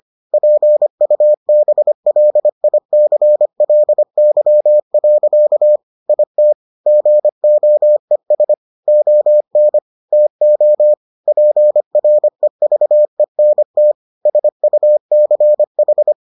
Morse code, heard as a keyed tone, about 25 wpm, 600 Hz. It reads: PUBLICLY. IT GOES ON TO PREVENT SUCH